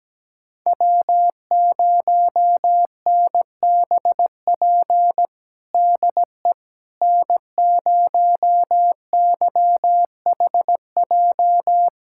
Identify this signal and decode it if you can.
Morse code, heard as a keyed tone, about 17 wpm, 700 Hz. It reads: W0NBP DE N0YHJ
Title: USA Callsign DE Callsign - 5 Characters - 17wpm